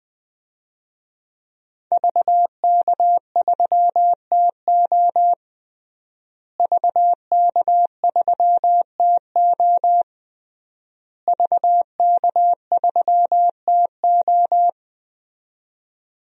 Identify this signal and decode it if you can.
Morse code, heard as a keyed tone, about 20 wpm, 700 Hz. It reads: VK3TO VK3TO VK3TO
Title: DXCC-100 Call Signs - 5 Characters - Repeated 3x - 20wpm